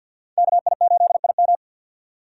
Morse code, heard as a keyed tone, about 52 wpm, 700 Hz. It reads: KI8IM
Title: USA Call Signs - 5 Characters - 2x Character Spacing - 50wpm